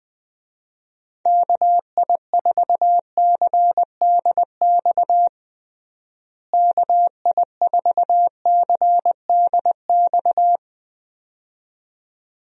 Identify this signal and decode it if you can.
Morse code, heard as a keyed tone, about 21 wpm, 700 Hz. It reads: KI4CDX KI4CDX
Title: USA Call Signs - 6 Characters - Repeated 2x - 20wpm